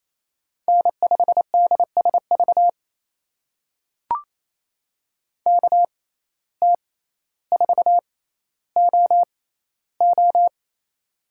Morse code, heard as a keyed tone, about 28 wpm, spaced as for 14 wpm, 700 Hz. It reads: N5DSV KT4OO
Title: USA Call Signs - 5 Characters - 6x Character Spacing - 28wpm